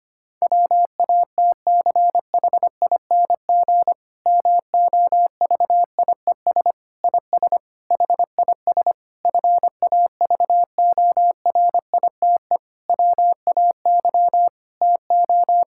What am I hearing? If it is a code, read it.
Morse code, heard as a keyed tone, about 25 wpm, 700 Hz. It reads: WATCHING MOVIES IS HIS FAVORITE WAY TO
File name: practice / 25wpm / Difficult Sentences with Hints - 25wpm